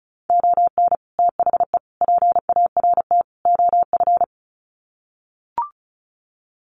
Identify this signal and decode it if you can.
Morse code, heard as a keyed tone, about 34 wpm, 700 Hz. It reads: ON THE PART OF
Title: Sets of 4 Words 35wpm